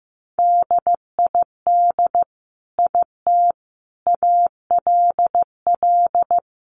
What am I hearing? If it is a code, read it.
Morse code, heard as a keyed tone, about 14 wpm, 700 Hz. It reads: DID IT ALL
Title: Sentences from Top 500 Words 15wpm